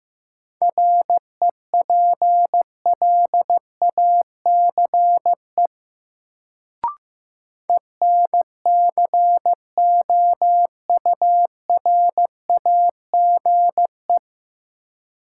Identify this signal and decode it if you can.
Morse code, heard as a keyed tone, about 15 wpm, 700 Hz. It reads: REPLACE ENCOURAGE
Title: Sentences from Top 1400 Words - New Word Drill - 15wpm